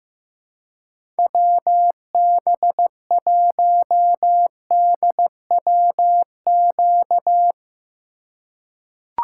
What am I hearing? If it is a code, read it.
Morse code, heard as a keyed tone, about 15 wpm, 700 Hz. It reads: WB1DWQ
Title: USA Call Signs - 6 Characters - 3x Character Spacing - 15wpm